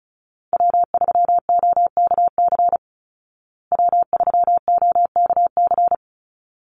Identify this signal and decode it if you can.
Morse code, heard as a keyed tone, about 34 wpm, 700 Hz. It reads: W3OKC W3OKC
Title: Call Signs Repeated 3x - 35wpm